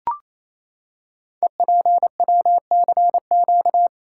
Morse code, heard as a keyed tone, about 27 wpm, 700 Hz. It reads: EPWCQ
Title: Code Groups - 5 Letter:Numbers - 28wpm